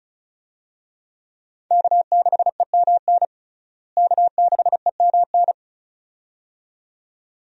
Morse code, heard as a keyed tone, about 36 wpm, 700 Hz. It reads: K6EMN K6EMN
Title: USA Call Signs - 5 Characters - Repeated 2x - 35wpm